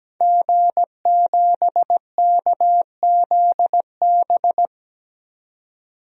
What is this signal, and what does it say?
Morse code, heard as a keyed tone, about 17 wpm, 700 Hz. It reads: G7KZB
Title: DXCC-100 Call Signs - 5 Characters - 17wpm